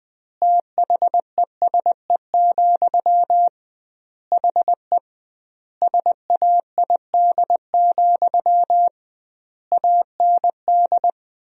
Morse code, heard as a keyed tone, about 20 wpm, 700 Hz. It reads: THESE, HE SAID, AND
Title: 2x Word-Spacing - Sentences from Top 300 Words - 20wpm